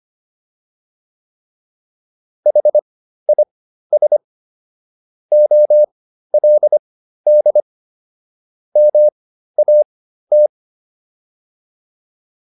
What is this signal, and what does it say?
Morse code, heard as a keyed tone, about 25 wpm, spaced as for 13 wpm, 600 Hz. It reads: HIS OLD MAT
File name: PR202 13